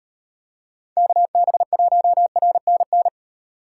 Morse code, heard as a keyed tone, about 38 wpm, 700 Hz. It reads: KB1RNN